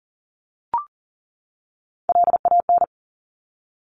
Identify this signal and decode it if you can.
Morse code, heard as a keyed tone, about 38 wpm, 700 Hz. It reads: LAN
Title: Top Three Letter Combinations in Words 40wpm